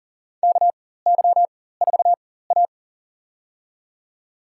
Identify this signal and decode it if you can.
Morse code, heard as a keyed tone, about 40 wpm, spaced as for 19 wpm, 700 Hz. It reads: KY4A
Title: USA Call Signs - 4 Characters - 4x Character Spacing - 40wpm